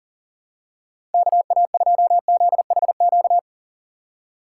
Morse code, heard as a keyed tone, about 40 wpm, 700 Hz. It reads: KA2ZHQ
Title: USA Call Signs - 6 Characters - 8x Character Spacing - 40wpm